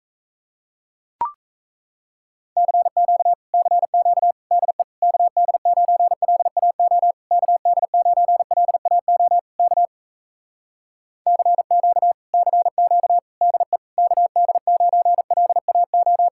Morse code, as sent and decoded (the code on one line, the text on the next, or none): -.-. --.- / -.-. --.- / -.. / -.- -.. ----. .-.. .- --- / -.- -.. ----. .-.. .- --- / -.- / -.-. --.- / -.-. --.- / -.. / -.- -.. ----. .-.. .- ---
CQ CQ DE KD9LAO KD9LAO K CQ CQ DE KD9LAO